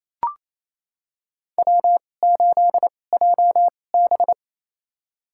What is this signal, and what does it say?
Morse code, heard as a keyed tone, about 28 wpm, spaced as for 20 wpm, 700 Hz. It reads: W8JB
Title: USA Call Signs - 4 Characters - 2x Character Spacing - 28wpm